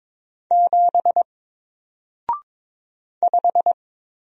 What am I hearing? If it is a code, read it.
Morse code, heard as a keyed tone, about 22 wpm, 700 Hz. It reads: 7 5
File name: Single-Letter-Number-22wpm